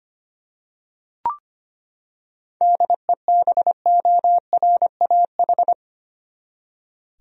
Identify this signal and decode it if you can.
Morse code, heard as a keyed tone, about 25 wpm, 700 Hz. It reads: DEBORAH
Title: USA Common Names - 25wpm